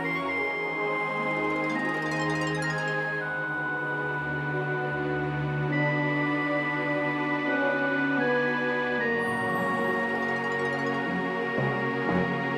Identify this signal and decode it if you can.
Dial-up modem handshake.